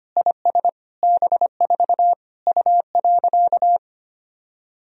Morse code, heard as a keyed tone, about 25 wpm, 700 Hz. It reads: IS B4 U.
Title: Sentences from Top 200 Words and Abbreviations 25wpm